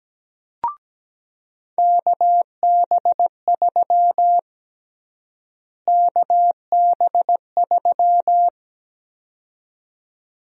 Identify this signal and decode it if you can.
Morse code, heard as a keyed tone, about 17 wpm, 700 Hz. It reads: KB3 KB3
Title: USA Call Sign Prefixes - Repeated 2x - 17wpm